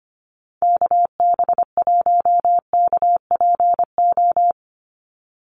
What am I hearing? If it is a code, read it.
Morse code, heard as a keyed tone, about 24 wpm, 700 Hz. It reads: KB1KPO
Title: USA Call Signs 25wpm